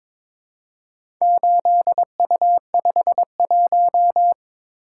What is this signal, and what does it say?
Morse code, heard as a keyed tone, about 22 wpm, 700 Hz. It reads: 8U51
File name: practice / 22wpm / Four Letter-Number Combinations - 22wpm